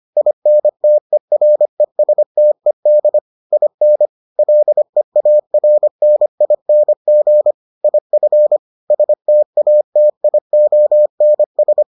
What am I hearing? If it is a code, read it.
Morse code, heard as a keyed tone, about 25 wpm, 600 Hz. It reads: INTERESTED IN LEARNING IF STATIONS